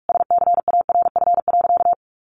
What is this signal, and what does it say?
Morse code, heard as a keyed tone, about 44 wpm, 700 Hz. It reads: SCARF.
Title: Easy Phrases 45wpm